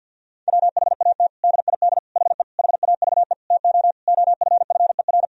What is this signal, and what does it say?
Morse code, heard as a keyed tone, about 50 wpm, 700 Hz. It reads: WHAT DID HE HAVE TO OFFER
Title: Sentences from Top 1000 Words - Review Part 4 - 50wpm